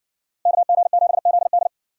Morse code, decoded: KG7ZD

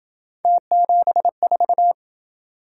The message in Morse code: T74

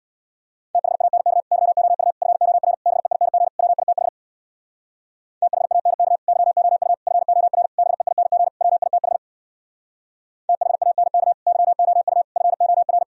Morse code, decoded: THANK YOU FOR BEING WITH THANK YOU FOR BEING WITH THANK YOU FOR

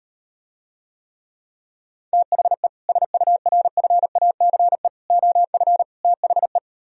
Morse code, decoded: THE SURFACE OF THE

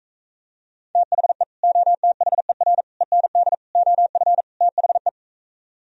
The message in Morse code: THE OTHER END OF THE

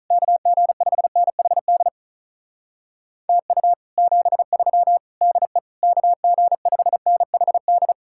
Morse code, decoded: KG5NHD TU 73 DE KG5NHD

